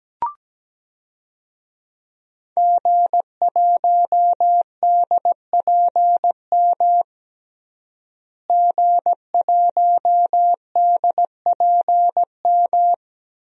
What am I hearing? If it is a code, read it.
Morse code, heard as a keyed tone, about 17 wpm, 700 Hz. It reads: G1DPM G1DPM
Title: DXCC-100 Call Signs - 5 Characters - Repeated 3x - 17wpm